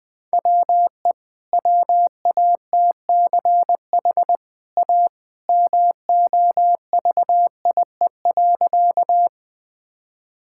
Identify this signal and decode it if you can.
Morse code, heard as a keyed tone, about 21 wpm, 700 Hz. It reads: WE WATCH A MOVIE.
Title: Easy Sentences with Hints - 20wpm